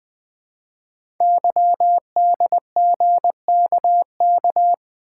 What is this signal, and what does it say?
Morse code, heard as a keyed tone, about 20 wpm, 700 Hz. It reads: YDGKK
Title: Code Groups - 5 Letters - 20wpm